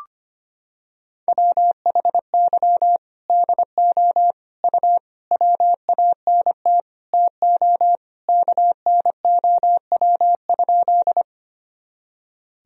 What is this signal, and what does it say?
Morse code, heard as a keyed tone, about 25 wpm, 700 Hz. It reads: WHY DO U WANT TO KNOW?
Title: Sentences from Top 200 Words and Abbreviations 25wpm